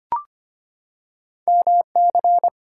Morse code, heard as a keyed tone, about 25 wpm, 690 Hz. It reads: MC
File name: Two Letter-Number Combinations - 25wpm